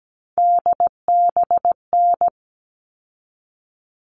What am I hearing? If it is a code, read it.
Morse code, heard as a keyed tone, about 17 wpm, 700 Hz. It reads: DBN
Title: All Cut Numbers - 3-Digit 17wpm